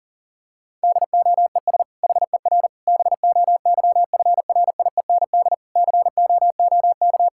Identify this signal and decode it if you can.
Morse code, heard as a keyed tone, about 40 wpm, 700 Hz. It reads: DOES HER BOYFRIEND COOK
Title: Sentences from Top 1000 Words - Review Part 5 - 40wpm